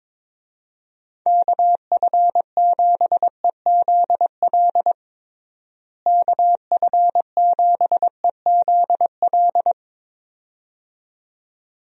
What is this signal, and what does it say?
Morse code, heard as a keyed tone, about 22 wpm, 700 Hz. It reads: KF7EZL KF7EZL
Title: USA Call Signs - 6 Characters - Repeated 2x - 22wpm